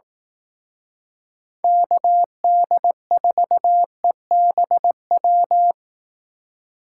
Morse code, heard as a keyed tone, about 18 wpm, 700 Hz. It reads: KD4EBW
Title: Speed Racing - USA 2x3 Call Signs 15wpm